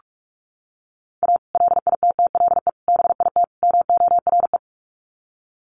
Morse code, heard as a keyed tone, about 44 wpm, 700 Hz. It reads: A LITTLE BIT MORE
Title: Sets of 4 Words 45wpm